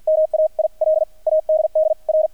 Morse code, heard as a keyed tone, about 48 wpm, spaced as for 40 wpm, 630 Hz, with radio noise. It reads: MAIL ADDR
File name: archived / CRSnet